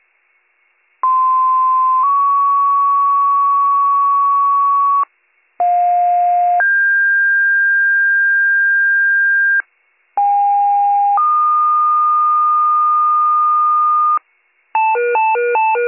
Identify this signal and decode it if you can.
Two-tone page, heard as a one-second tone, then a three-second tone, about 1040 then 1120 Hz, about 710 then 1600 Hz, about 810 then 1220 Hz. It is heard from 1.0 to 14.2 s.